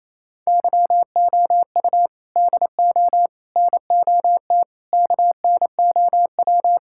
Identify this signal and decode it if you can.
Morse code, heard as a keyed tone, about 27 wpm, 700 Hz. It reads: YOU DO NOT KNOW
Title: Sentences from Top 1000 Words - Review Part 6 - 28wpm